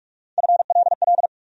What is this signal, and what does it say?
Morse code, heard as a keyed tone, about 46 wpm, 700 Hz. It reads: FLL
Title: USA Call Sign Postfixes - 3 Letters 45wpm